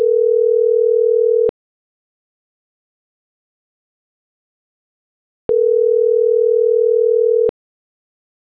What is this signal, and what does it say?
Ringback tone (North America), 440 + 480 Hz.
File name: CENTRAL GENERAL / out ricardo.diaz